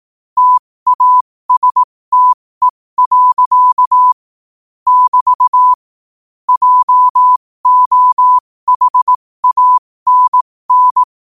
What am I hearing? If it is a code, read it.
Morse code, heard as a keyed tone, about 18 wpm, spaced as for 15 wpm, 1000 Hz. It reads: TASTE. <BT> JOHANN
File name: qotdfiles / 15wpm